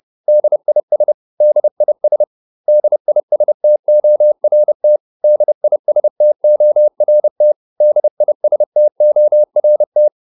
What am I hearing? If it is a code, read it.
Morse code, heard as a keyed tone, about 30 wpm, 600 Hz. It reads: DIS DIS DISTORT DISTORT DISTORT